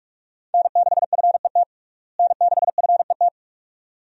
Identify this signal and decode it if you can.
Morse code, heard as a keyed tone, about 44 wpm, 700 Hz. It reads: N6FET N6FET
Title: USA Call Signs - 5 Characters - Repeated 2x - 45wpm